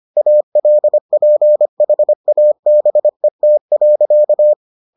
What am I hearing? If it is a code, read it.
Morse code, heard as a keyed tone, about 25 wpm, 600 Hz. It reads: ALPHABET.